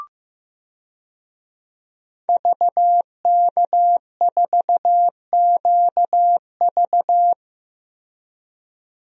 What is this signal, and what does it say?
Morse code, heard as a keyed tone, about 16 wpm, 700 Hz. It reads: VK4QV
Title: DXCC-100 Call Signs - 5 Characters - Repeated 3x - 15wpm